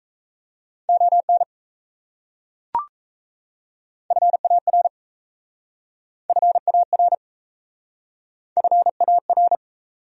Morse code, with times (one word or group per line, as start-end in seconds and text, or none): 0.9-1.4 ON
4.1-4.9 FAR
6.3-7.1 FAR
8.6-9.6 FAR